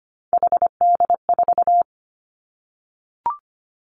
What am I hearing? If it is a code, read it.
Morse code, heard as a keyed tone, about 25 wpm, 700 Hz. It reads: HD4
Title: Three Challenging Letter-Number Combinations 25wpm